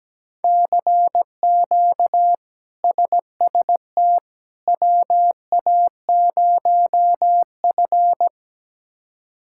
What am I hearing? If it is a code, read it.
Morse code, heard as a keyed tone, about 17 wpm, 700 Hz. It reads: CQ SST WA0F
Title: K1USN SST QSO Practice - 17wpm